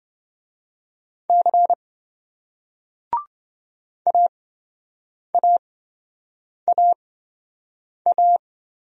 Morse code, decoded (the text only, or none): C AAAA